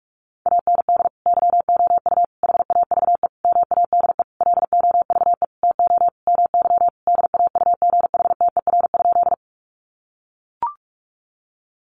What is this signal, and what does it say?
Morse code, heard as a keyed tone, about 44 wpm, 700 Hz. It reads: AND YOU HAVE MADE LOVE TO MY DAUGHTER?
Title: Sentences from Top 1500 Words 45wpm